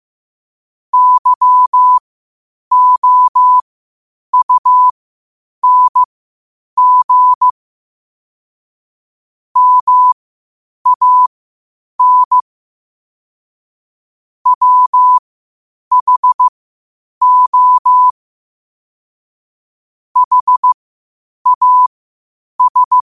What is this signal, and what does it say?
Morse code, heard as a keyed tone, about 15 wpm, spaced as for 8 wpm, 1000 Hz. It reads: YOUNG MAN WHO HAS
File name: qotdfiles / 8wpm